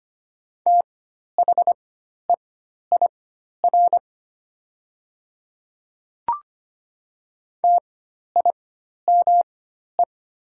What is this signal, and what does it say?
Morse code, heard as a keyed tone, about 25 wpm, spaced as for 12 wpm, 700 Hz. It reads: THEIR TIME
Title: 4x Character Spacing - Top 200 Words - 25wpm